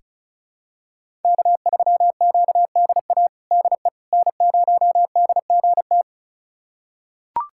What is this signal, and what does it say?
Morse code, heard as a keyed tone, about 36 wpm, 700 Hz. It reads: K3QDA DE N0DGT
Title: USA Callsign DE Callsign - 5 Characters - 35wpm